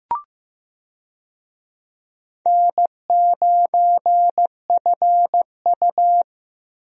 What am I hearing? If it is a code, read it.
Morse code, heard as a keyed tone, about 15 wpm, 700 Hz. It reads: N9FU
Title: USA Call Signs - 4 Characters - Repeated 3x - 15wpm